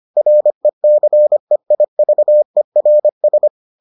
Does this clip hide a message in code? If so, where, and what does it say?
Yes: Morse code from 0.2 to 3.5 s, heard as a keyed tone, about 25 wpm, 600 Hz. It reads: RECEIVERS